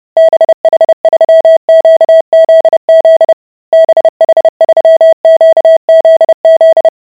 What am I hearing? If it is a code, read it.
Morse code, heard as a keyed tone, about 30 wpm, 650 Hz. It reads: BH3QZZ BH3QZZ